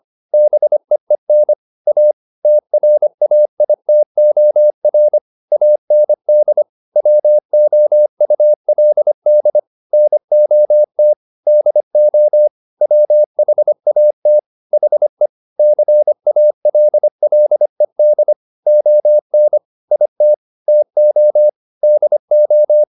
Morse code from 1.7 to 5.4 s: A TRAITOR

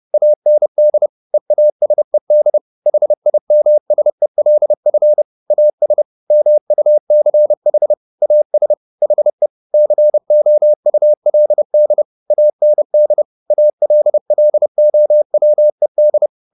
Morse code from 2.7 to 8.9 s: HIMSELF AS MUCH AS

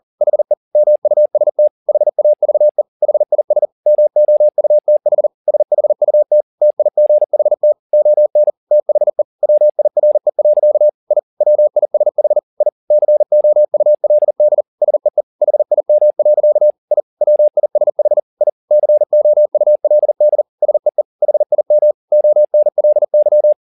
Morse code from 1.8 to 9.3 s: HAVE HIS MOUTH SHUT TIGHT ON THE

